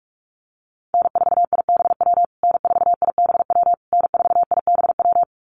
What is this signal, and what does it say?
Morse code, heard as a keyed tone, about 44 wpm, 700 Hz. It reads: N4IBW N4IBW N4IBW
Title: USA Call Signs Repeated 3x 45wpm